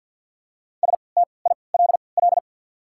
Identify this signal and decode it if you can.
Morse code, heard as a keyed tone, about 53 wpm, spaced as for 29 wpm, 690 Hz. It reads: STILL
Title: 3x Character Spacing - Top 500 Words - 50wpm